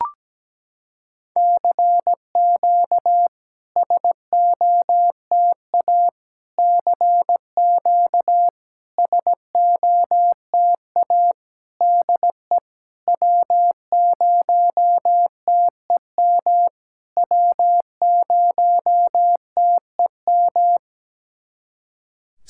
Morse code, heard as a keyed tone, about 17 wpm, 700 Hz. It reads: CQ SOTA CQ SOTA DE W0TEM W0TEM
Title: SOTA QSO Practice 17wpm